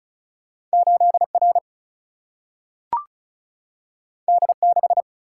Morse code, heard as a keyed tone, about 36 wpm, 700 Hz. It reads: 8 R D 6
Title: Single-Letter Single-Number 35wpm